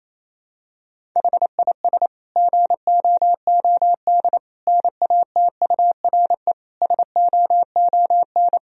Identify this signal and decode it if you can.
Morse code, heard as a keyed tone, about 27 wpm, 700 Hz. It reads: HIS GOOD NATURE SOON
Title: Sentences from Top 1000 Words - Review Part 1 - 28wpm